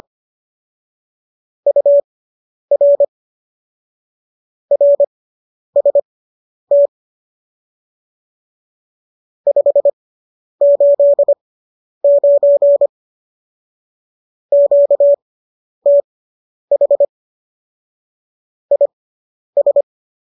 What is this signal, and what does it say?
Morse code, heard as a keyed tone, about 25 wpm, spaced as for 10 wpm, 570 Hz. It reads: UR RST 589 QTH IS